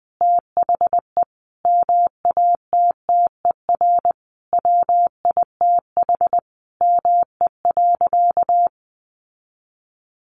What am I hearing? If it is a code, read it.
Morse code, heard as a keyed tone, about 20 wpm, 700 Hz. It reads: THE MATTER WITH ME.
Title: Sentences from Top 700 Words 20wpm